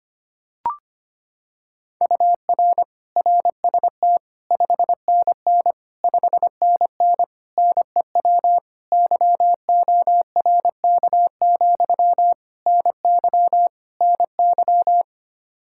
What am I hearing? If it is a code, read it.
Morse code, heard as a keyed tone, about 25 wpm, 700 Hz. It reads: UR RST 5NN 5NN NEW YORK, NY NY
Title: Practice QSO - Exchange 1 - Beginning - 25wpm